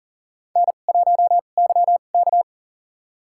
Morse code, heard as a keyed tone, about 40 wpm, spaced as for 29 wpm, 700 Hz. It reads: N1YK
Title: USA Call Signs - 4 Characters - 2x Character Spacing - 40wpm